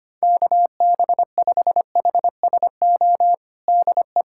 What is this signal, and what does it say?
Morse code, heard as a keyed tone, about 25 wpm, 700 Hz. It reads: KB5HSO DE
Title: USA Callsign DE Callsign 25wpm